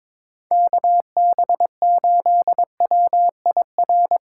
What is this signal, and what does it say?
Morse code, heard as a keyed tone, about 25 wpm, 700 Hz. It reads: KB8WIR
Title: Speed Racing - USA 2x3 Call Signs 22wpm